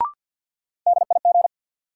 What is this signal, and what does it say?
Morse code, heard as a keyed tone, about 52 wpm, 700 Hz. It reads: DIG
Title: Sentences from Top 1600 Words - New Word Drill - 50wpm